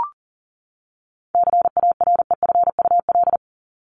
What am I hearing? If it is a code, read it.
Morse code, heard as a keyed tone, about 40 wpm, 700 Hz. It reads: CAREFUL
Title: Top 1501-1750 Words 40wpm